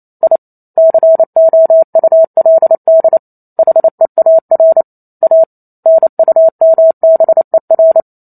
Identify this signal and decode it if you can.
Morse code, heard as a keyed tone, about 28 wpm, 650 Hz. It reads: I COULD HEAR A NUMBER